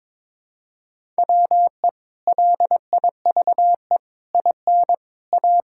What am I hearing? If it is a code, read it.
Morse code, heard as a keyed tone, about 21 wpm, 700 Hz. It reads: WE LIVE IN A
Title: Sentences from Top 1000 Words - Review Part 5 - 22wpm